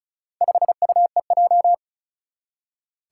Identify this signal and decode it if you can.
Morse code, heard as a keyed tone, about 36 wpm, 700 Hz. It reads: 5UEJ
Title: Four Challenging Letter-Number Combinations 35wpm